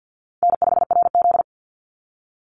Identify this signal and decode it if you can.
Morse code, heard as a keyed tone, about 50 wpm, 700 Hz. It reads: N5RZ
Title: Easy Call Signs - 50wpm